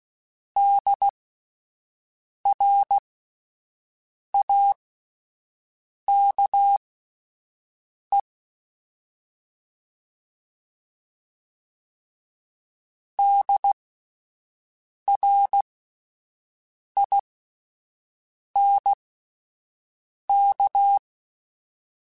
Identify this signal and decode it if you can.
Morse code, heard as a keyed tone, about 16 wpm, spaced as for 5 wpm, 800 Hz. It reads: DRAKE DRINK